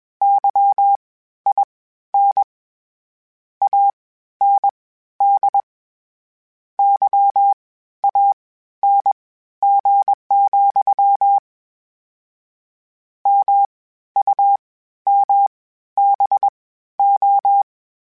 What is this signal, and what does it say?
Morse code, heard as a keyed tone, about 21 wpm, spaced as for 13 wpm, 800 Hz. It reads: YIN AND YANG, MUMBO
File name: Misc / Misc-mp3s / WordPairs sp @20